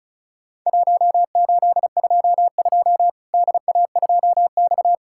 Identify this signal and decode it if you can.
Morse code, heard as a keyed tone, about 36 wpm, 700 Hz. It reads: 1822 DA2X
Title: Worked All Europe DX Contest - QTC Exchange - 35wpm